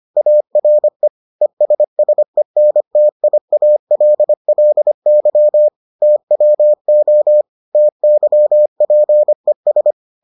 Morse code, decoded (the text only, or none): ARE ESSENTIALLY TWO TYPES